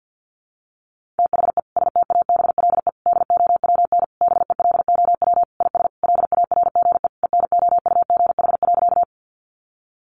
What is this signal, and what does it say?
Morse code, heard as a keyed tone, about 50 wpm, 700 Hz. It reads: THE STABLE DOWN BELOW IS LARGE ENOUGH.
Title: Sentences from Top 1500 Words 50wpm